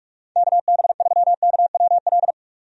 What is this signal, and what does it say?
Morse code, heard as a keyed tone, about 47 wpm, 700 Hz. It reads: KB3KWL
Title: USA Call Signs - 6 Characters - 45wpm